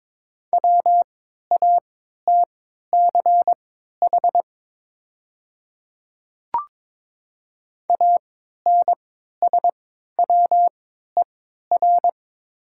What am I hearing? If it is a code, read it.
Morse code, heard as a keyed tone, about 23 wpm, spaced as for 13 wpm, 700 Hz. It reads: WATCH ANSWER
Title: 3x Character Spacing - Top 300 Words - 22wpm